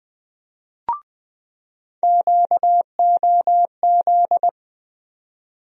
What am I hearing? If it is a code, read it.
Morse code, heard as a keyed tone, about 20 wpm, 700 Hz. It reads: QOZ
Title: USA Call Sign Postfixes - 3 Letters 20wpm